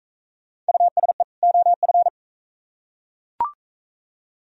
Morse code, heard as a keyed tone, about 43 wpm, 700 Hz. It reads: USE OF